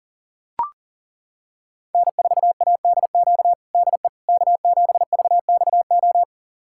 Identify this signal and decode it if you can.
Morse code, heard as a keyed tone, about 40 wpm, 700 Hz. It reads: N4ADQ DE K7VXO
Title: USA Callsign DE Callsign - 5 Characters - 40wpm